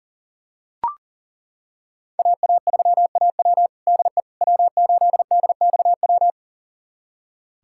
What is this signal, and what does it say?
Morse code, heard as a keyed tone, about 40 wpm, 700 Hz. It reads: AA3AW DE W8DXW
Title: USA Callsign DE Callsign - 5 Characters - 40wpm